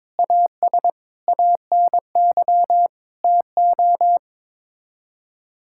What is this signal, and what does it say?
Morse code, heard as a keyed tone, about 21 wpm, 700 Hz. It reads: AS ANY TO
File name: practice / 22wpm / Sets of 7 Words - 22wpm